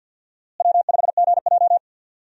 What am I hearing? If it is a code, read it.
Morse code, heard as a keyed tone, about 46 wpm, 700 Hz. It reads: W5GJ
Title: USA Call Signs - 4 Characters - Repeated 3x - 50wpm